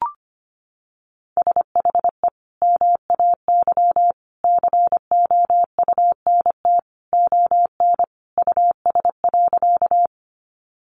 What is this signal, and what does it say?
Morse code, heard as a keyed tone, about 25 wpm, 700 Hz. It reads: SHE MAY COUNT ON US.
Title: Sentences from Top 1500 Words 25wpm